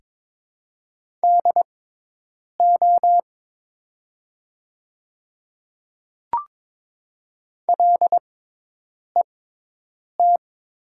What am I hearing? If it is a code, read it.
Morse code, heard as a keyed tone, about 22 wpm, spaced as for 8 wpm, 700 Hz. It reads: DO LET
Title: 6x Character Spacing - Top 300 Words - 22wpm